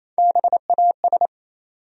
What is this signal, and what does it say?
Morse code, heard as a keyed tone, about 28 wpm, 700 Hz. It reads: BAS
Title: USA Call Sign Postfixes - 3 Letters 28wpm